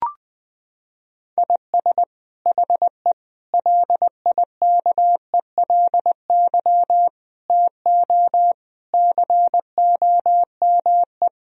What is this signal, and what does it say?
Morse code, heard as a keyed tone, about 20 wpm, 700 Hz. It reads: IS HE LIKELY TO COME
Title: Sentences from Top 1000 Words - Review Part 4 - 20wpm